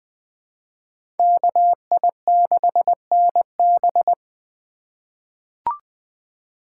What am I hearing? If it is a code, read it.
Morse code, heard as a keyed tone, about 21 wpm, 700 Hz. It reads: KI6NB
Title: USA Call Signs - 5 Characters - 3x Character Spacing - 20wpm